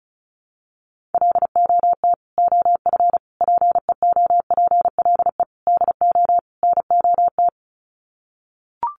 Morse code, heard as a keyed tone, about 34 wpm, 700 Hz. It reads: LOT OF PEOPLE DO NOT
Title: Sets of 5 Words Encore 35wpm